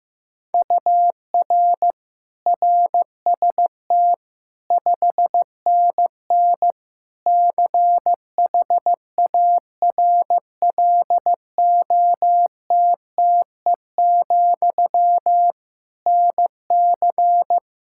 Morse code, heard as a keyed tone, about 16 wpm, 700 Hz. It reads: UR RST 5NN CHARLOTTE, NC
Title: K1USN SST QSO Practice 15wpm